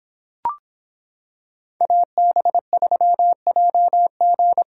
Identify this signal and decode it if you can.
Morse code, heard as a keyed tone, about 26 wpm, 700 Hz. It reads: AB3JG